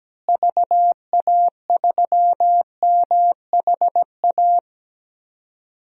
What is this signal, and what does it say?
Morse code, heard as a keyed tone, about 17 wpm, 700 Hz. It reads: VA3MHA
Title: DXCC-100 Call Signs - 6 Characters - Repeated 2x - 17wpm